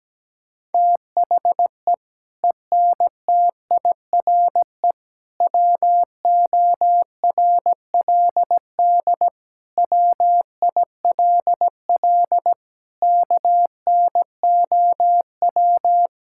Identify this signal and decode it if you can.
Morse code, heard as a keyed tone, about 17 wpm, 700 Hz. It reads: THE ENTIRE WORLD WILL KNOW
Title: Sentences from Top 1000 Words - Review Part 5 - 17wpm